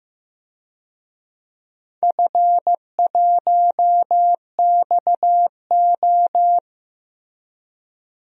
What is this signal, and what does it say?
Morse code, heard as a keyed tone, about 15 wpm, 700 Hz. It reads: F1XO
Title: DXCC-100 Call Signs - 4 Characters - 15wpm